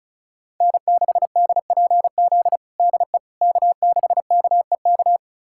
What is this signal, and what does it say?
Morse code, heard as a keyed tone, about 34 wpm, 700 Hz. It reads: N6DPZ DE K6KEK
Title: USA Callsign DE Callsign - 5 Characters - 35wpm